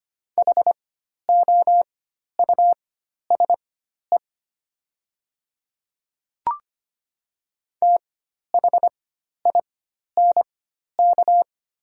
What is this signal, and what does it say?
Morse code, heard as a keyed tone, about 25 wpm, spaced as for 12 wpm, 700 Hz. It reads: HOUSE THINK
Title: 4x Character Spacing - Top 300 Words - 25wpm